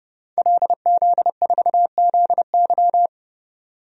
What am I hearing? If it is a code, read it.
Morse code, heard as a keyed tone, about 30 wpm, 700 Hz. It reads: LZ4ZY
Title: DXCC-100 Call Signs - 5 Characters - 30wpm